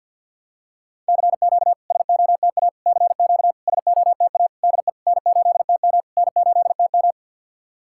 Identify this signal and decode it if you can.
Morse code, heard as a keyed tone, about 50 wpm, 700 Hz. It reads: CQ SOTA CQ SOTA DE N8TM N8TM
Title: SOTA QSO Practice 50wpm